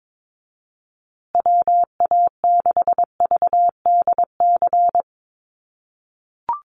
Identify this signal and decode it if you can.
Morse code, heard as a keyed tone, about 22 wpm, 700 Hz. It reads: WA6VDC
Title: USA Call Signs Repeated 2x 22wpm